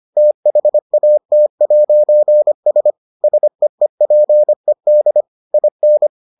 Morse code, heard as a keyed tone, about 25 wpm, 600 Hz. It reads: THAT'S SEEPED IN